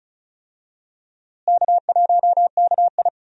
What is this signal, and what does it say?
Morse code, heard as a keyed tone, about 36 wpm, 700 Hz. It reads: K1KI
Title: USA Call Signs - 4 Characters - Repeated 2x - 35wpm